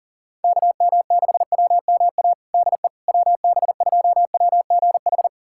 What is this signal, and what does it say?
Morse code, heard as a keyed tone, about 37 wpm, 700 Hz. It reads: KM6WMA DE WB2WGH
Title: USA Callsign DE Callsign - 6 Characters - 40wpm